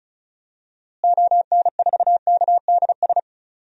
Morse code, heard as a keyed tone, about 36 wpm, 700 Hz. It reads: ON4KDS
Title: DXCC-100 Call Signs - 6 Characters - Repeated 3x - 35wpm